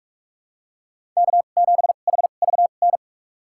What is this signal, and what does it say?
Morse code, heard as a keyed tone, about 46 wpm, spaced as for 32 wpm, 700 Hz. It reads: K7HVN
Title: USA Call Signs - 2x Character Spacing - 45wpm